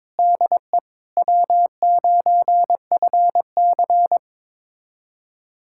Morse code, heard as a keyed tone, about 21 wpm, 700 Hz. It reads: DE W9FC
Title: USA Callsign DE Callsign - 4 Characters - 22wpm